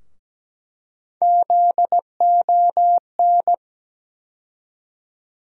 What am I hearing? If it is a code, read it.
Morse code, heard as a keyed tone, about 17 wpm, 700 Hz. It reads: ZON
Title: USA Call Sign Postfixes - 3 Letters 17wpm